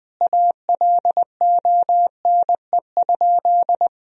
Morse code, heard as a keyed tone, about 20 wpm, 700 Hz. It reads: ALONE?
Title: Sentences from Top 1000 Words - Review Part 3 - 20wpm